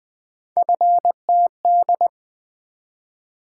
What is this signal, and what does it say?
Morse code, heard as a keyed tone, about 20 wpm, 700 Hz. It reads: FTD
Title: Three Letter-Number Combinations - 20wpm